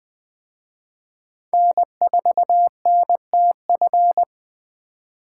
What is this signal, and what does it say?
Morse code, heard as a keyed tone, about 20 wpm, 700 Hz. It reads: N4NTF